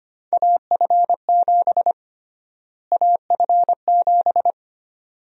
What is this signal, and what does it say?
Morse code, heard as a keyed tone, about 25 wpm, 700 Hz. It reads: AF7 AF7
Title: USA Call Sign Prefixes - Repeated 2x - 25wpm